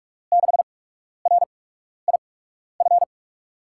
Morse code, heard as a keyed tone, about 44 wpm, spaced as for 12 wpm, 700 Hz. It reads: 6RIF
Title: USA Call Signs - 8x Character Spacing - 45wpm